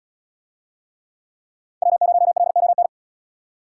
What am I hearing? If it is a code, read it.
Morse code, heard as a keyed tone, about 77 wpm, 700 Hz. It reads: W0LON